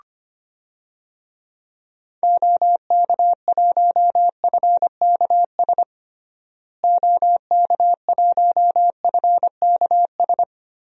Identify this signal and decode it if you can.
Morse code, heard as a keyed tone, about 25 wpm, 700 Hz. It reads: OK1FKS OK1FKS
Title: DXCC-100 Call Signs - 6 Characters - Repeated 2x - 25wpm